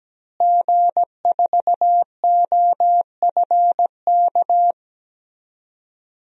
Morse code, heard as a keyed tone, about 17 wpm, 700 Hz. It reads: G4OFK
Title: DXCC-100 Call Signs - 5 Characters - Repeated 3x - 17wpm